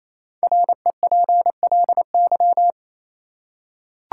Morse code, heard as a keyed tone, about 28 wpm, 700 Hz. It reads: REPLY
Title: Five Letter Words - 28wpm